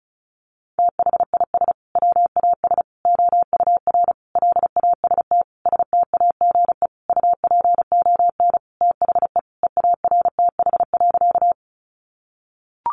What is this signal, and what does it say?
Morse code, heard as a keyed tone, about 34 wpm, 700 Hz. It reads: THIS WAS OUR LAST STAGE UPON THE EARTH.
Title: Sentences from Top 1200 Words 35wpm